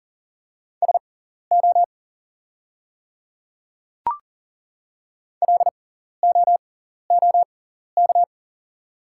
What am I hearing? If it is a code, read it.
Morse code, heard as a keyed tone, about 40 wpm, spaced as for 14 wpm, 690 Hz. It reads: SO LOOK
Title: 6x Character Spacing - Top 100 Words - 40wpm